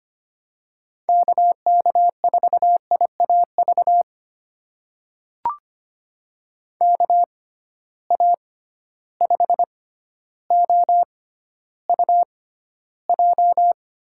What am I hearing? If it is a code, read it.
Morse code, heard as a keyed tone, about 25 wpm, spaced as for 13 wpm, 700 Hz. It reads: KK4IAV KA5OUJ
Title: USA Call Signs - 6 Characters - 6x Character Spacing - 25wpm